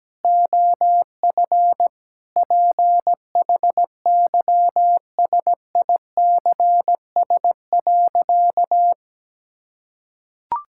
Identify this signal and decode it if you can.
Morse code, heard as a keyed tone, about 17 wpm, 700 Hz. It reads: OF PHYSICS.
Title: Sentences from Top 1000 Words - Review Part 5 - 17wpm